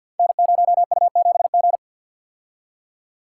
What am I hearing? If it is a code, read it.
Morse code, heard as a keyed tone, about 51 wpm, 700 Hz. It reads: N0U7G
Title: Code Groups - 5 Letter:Numbers - 50wpm